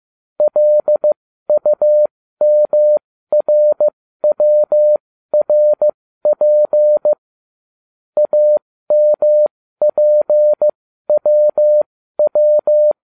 Morse code, heard as a keyed tone, about 15 wpm, spaced as for 13 wpm, 600 Hz. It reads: LUMRWRP AMPWW